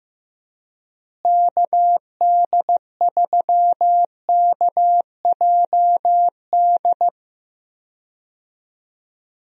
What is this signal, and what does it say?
Morse code, heard as a keyed tone, about 15 wpm, 700 Hz. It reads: KD3KJD
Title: USA Call Signs - 6 Characters - Repeated 2x - 15wpm